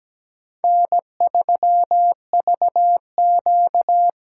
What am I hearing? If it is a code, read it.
Morse code, heard as a keyed tone, about 17 wpm, 700 Hz. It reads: N3VQ
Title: USA Call Signs - 4 Characters - 3x Character Spacing - 17wpm